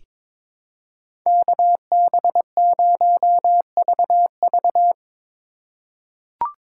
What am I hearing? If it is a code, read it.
Morse code, heard as a keyed tone, about 22 wpm, 700 Hz. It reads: KB0VV
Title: USA Call Signs - 4x Character Spacing - 22wpm